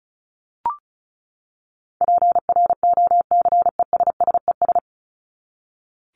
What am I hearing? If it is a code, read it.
Morse code, heard as a keyed tone, about 34 wpm, 700 Hz. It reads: PROCESSES
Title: ES:S Suffix - Top 100 Words with Base Words 35wpm